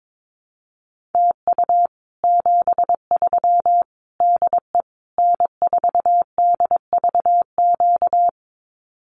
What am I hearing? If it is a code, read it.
Morse code, heard as a keyed tone, about 22 wpm, 700 Hz. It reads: TU 73 DE N4DVQ
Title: Common QSO Phrases Encore 22wpm